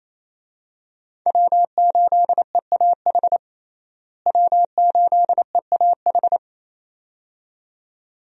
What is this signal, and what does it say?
Morse code, heard as a keyed tone, about 27 wpm, 700 Hz. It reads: W8EAH W8EAH
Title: USA Call Signs - 5 Characters - Repeated 2x - 28wpm